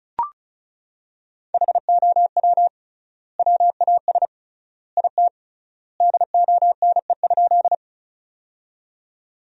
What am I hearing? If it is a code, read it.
Morse code, heard as a keyed tone, about 34 wpm, 700 Hz. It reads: HOW WAS IT DONE?
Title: 3x Word-Spacing - Sentences from Top 500 Words - 35wpm